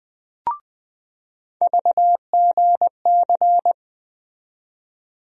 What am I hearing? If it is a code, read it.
Morse code, heard as a keyed tone, about 20 wpm, 700 Hz. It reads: VGC
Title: USA Call Sign Postfixes - 3 Letters 20wpm